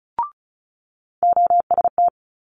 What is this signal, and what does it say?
Morse code, heard as a keyed tone, about 34 wpm, 700 Hz. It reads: OST
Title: Multi-letter Phonemes Quick Fire 35wpm